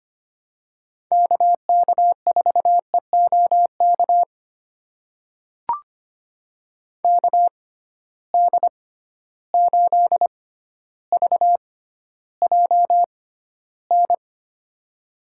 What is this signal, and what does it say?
Morse code, heard as a keyed tone, about 25 wpm, spaced as for 13 wpm, 700 Hz. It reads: KK4EOK KD8VJN
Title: USA Call Signs - 6 Characters - 6x Character Spacing - 25wpm